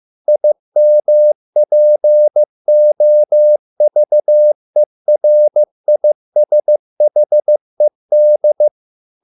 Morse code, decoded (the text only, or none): IMPOVERISHED